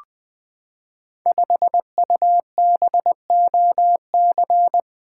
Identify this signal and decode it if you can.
Morse code, heard as a keyed tone, about 20 wpm, 700 Hz. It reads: 5UBOC
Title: Code Groups - 5 Letter:Numbers - 20wpm